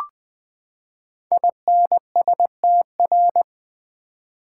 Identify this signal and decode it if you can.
Morse code, heard as a keyed tone, about 20 wpm, 700 Hz. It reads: INSTR
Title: Build A Word - 5 Letters At A Time - 12-Letter Words - 20wpm